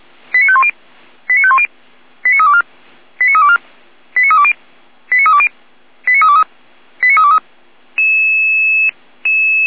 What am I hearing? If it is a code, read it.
Five-tone selective call (ZVEI), heard as quick runs of five tones, tones of about 70 ms. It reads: 89510 89510 89324 89324 89320 89320 89323 89323